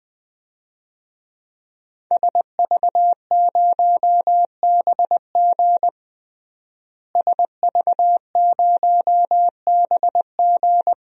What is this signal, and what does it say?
Morse code, heard as a keyed tone, about 20 wpm, 700 Hz. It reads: SV0BG SV0BG
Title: DXCC-100 Call Signs - 5 Characters - Repeated 3x - 20wpm